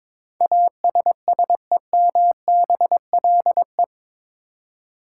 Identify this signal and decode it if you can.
Morse code, heard as a keyed tone, about 22 wpm, 700 Hz. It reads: ASSEMBLE
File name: practice / 22wpm / Eight Letter Words - 22wpm